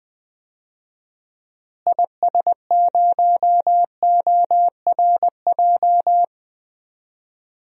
Morse code, heard as a keyed tone, about 21 wpm, 700 Hz. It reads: IS0ORJ
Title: DXCC-100 Call Signs - 6 Characters - Repeated 2x - 20wpm